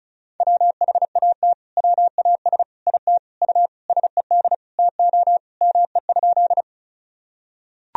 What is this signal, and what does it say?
Morse code, heard as a keyed tone, about 34 wpm, 700 Hz. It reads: WHAT WAS IT U SED TO ME?
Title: Sentences from Top 200 Words and Abbreviations 35wpm